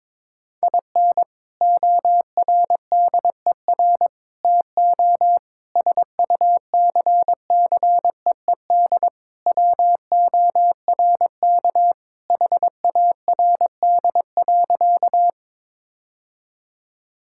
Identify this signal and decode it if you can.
Morse code, heard as a keyed tone, about 22 wpm, 700 Hz. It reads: IN ORDER TO SUCCEED WORK HARD.
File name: practice / 22wpm / Sentences with 3-Word Phrases - 22wpm